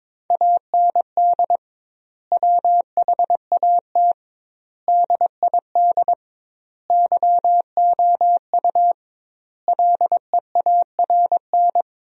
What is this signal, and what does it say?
Morse code, heard as a keyed tone, about 22 wpm, 700 Hz. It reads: AND WHAT DID YOU LEARN
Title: 2x Word-Spacing - Sentences from Top 300 Words - 22wpm